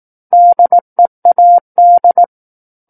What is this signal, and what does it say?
Morse code, heard as a keyed tone, about 18 wpm, 700 Hz. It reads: DEAD
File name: Book One / 09-Chapter 9 - 18 WPM